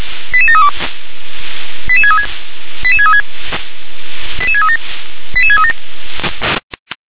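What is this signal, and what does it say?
Five-tone selective call (ZVEI), heard as quick runs of five tones, tones of about 70 ms. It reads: 88732 88636 88636 88637 88637
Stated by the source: ZVEI five-tone selective call received on SDR.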